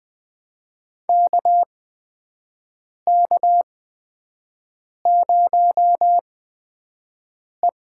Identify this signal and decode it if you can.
Morse code, heard as a keyed tone, about 20 wpm, spaced as for 5 wpm, 700 Hz. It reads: KK0E
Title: USA Call Signs - 4 Characters - 8x Character Spacing - 20wpm